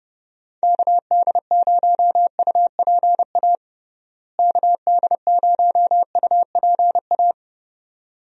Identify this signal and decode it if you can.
Morse code, heard as a keyed tone, about 30 wpm, 700 Hz. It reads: KD0UPA KD0UPA
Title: USA Call Signs - 6 Characters - Repeated 2x - 30wpm